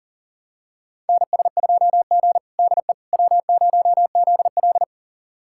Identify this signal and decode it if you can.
Morse code, heard as a keyed tone, about 37 wpm, 700 Hz. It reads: NS2G DE W0ZL
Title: USA Callsign DE Callsign - 4 Characters - 40wpm